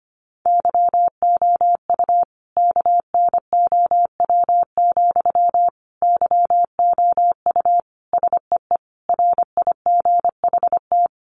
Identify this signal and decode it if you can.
Morse code, heard as a keyed tone, about 25 wpm, 700 Hz. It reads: YOU KNOW, YOU SEE RIGHT
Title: Sentences from Top 200 Words 25wpm